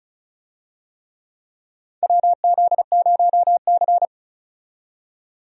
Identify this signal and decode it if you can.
Morse code, heard as a keyed tone, about 36 wpm, 700 Hz. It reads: WZ0C
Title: USA Call Signs - 4 Characters - 35wpm